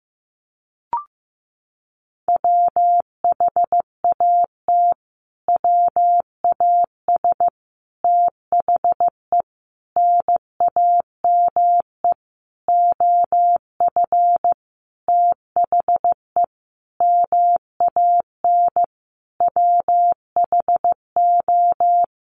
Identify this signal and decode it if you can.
Morse code, heard as a keyed tone, about 15 wpm, 700 Hz. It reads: WHAT WAS THE NAME OF THE MAN WHO
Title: Sentences from Top 300 Words 15wpm